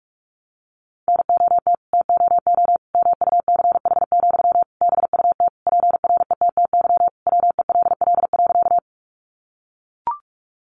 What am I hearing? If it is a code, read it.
Morse code, heard as a keyed tone, about 44 wpm, 700 Hz. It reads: NOT TOO MUCH, BUT PRETTY WELL.
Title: Sentences from Top 1000 Words 45wpm